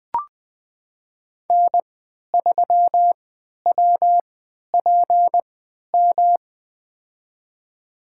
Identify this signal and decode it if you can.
Morse code, heard as a keyed tone, about 20 wpm, spaced as for 11 wpm, 700 Hz. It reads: N3WPM
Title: USA Call Signs - 3x Character Spacing - 20wpm